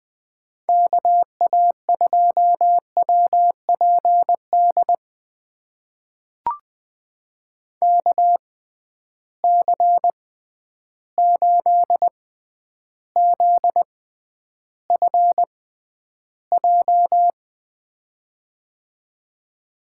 Morse code, heard as a keyed tone, about 20 wpm, spaced as for 10 wpm, 700 Hz. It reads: KA2WPD KC8ZFJ
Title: USA Call Signs - 6 Characters - 6x Character Spacing - 20wpm